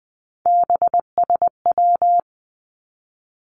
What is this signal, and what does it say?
Morse code, heard as a keyed tone, about 20 wpm, 700 Hz. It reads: BSW
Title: Three Challenging Letter-Number Combinations 20wpm